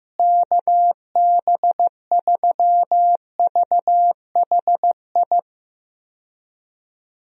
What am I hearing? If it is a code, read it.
Morse code, heard as a keyed tone, about 15 wpm, 700 Hz. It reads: KB3VHI
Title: USA Call Signs - 6 Characters - Repeated 2x - 15wpm